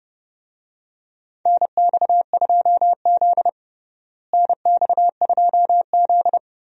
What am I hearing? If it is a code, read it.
Morse code, heard as a keyed tone, about 30 wpm, 700 Hz. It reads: NX2Z NX2Z
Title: USA Call Signs - 4 Characters - Repeated 3x - 30wpm